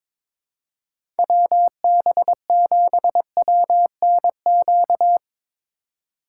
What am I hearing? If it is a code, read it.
Morse code, heard as a keyed tone, about 22 wpm, 700 Hz. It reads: WB7WNQ
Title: USA Call Signs - 6 Characters - 4x Character Spacing - 22wpm